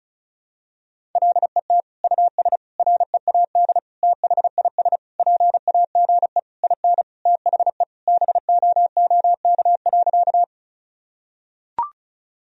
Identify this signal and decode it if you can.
Morse code, heard as a keyed tone, about 36 wpm, 700 Hz. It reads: LET US READ THIS PAGE IN THE BOOK.
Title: Sentences from Top 1000 Words - Review Part 1 - 35wpm